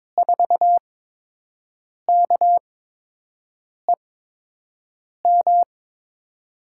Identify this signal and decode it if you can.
Morse code, heard as a keyed tone, about 22 wpm, spaced as for 6 wpm, 700 Hz. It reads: 4KEM
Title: USA Call Signs - 6 Characters - 8x Character Spacing - 22wpm